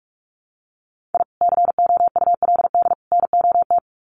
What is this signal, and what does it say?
Morse code, heard as a keyed tone, about 44 wpm, 700 Hz. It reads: I COULD NOT
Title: Sets of 3 Words 45wpm